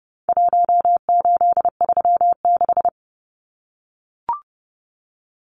Morse code, heard as a keyed tone, about 30 wpm, 700 Hz. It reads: 1836
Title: Numbers - Contest Style - 4-Digit 30wpm